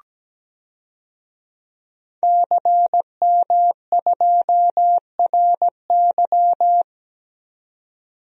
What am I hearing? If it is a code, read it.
Morse code, heard as a keyed tone, about 17 wpm, 700 Hz. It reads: CM2RY
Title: DXCC-100 Call Signs - 5 Characters - 17wpm